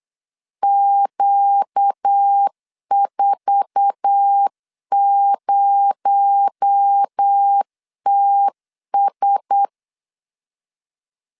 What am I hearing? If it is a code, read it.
Morse code, heard as a keyed tone, about 8 wpm, 800 Hz. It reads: Q40TS
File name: LEZIONI / 8 YJ45 / 8 YJ4 8wpm 3246